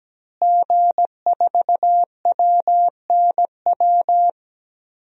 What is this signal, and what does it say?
Morse code, heard as a keyed tone, about 17 wpm, 700 Hz. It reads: G4WNW
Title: DXCC-100 Call Signs - 5 Characters - Repeated 2x - 17wpm